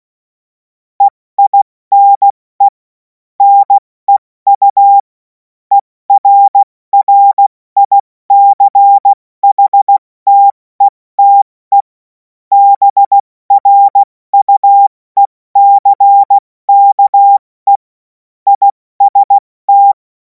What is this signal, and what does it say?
Morse code, heard as a keyed tone, about 16 wpm, 800 Hz. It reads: EINE NEU ERRICHTETE BRUECKE IST
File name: Morsekurs / Witze / Witz 10. Februar 2026 70 F80